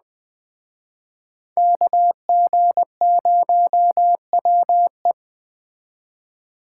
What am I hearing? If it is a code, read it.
Morse code, heard as a keyed tone, about 20 wpm, 700 Hz. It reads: KG0WE
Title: Speed Racing - USA 2x2 Call Signs 17wpm